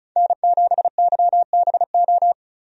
Morse code, heard as a keyed tone, about 36 wpm, 700 Hz. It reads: N7YBO
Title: USA Call Signs - 5 Characters - 35wpm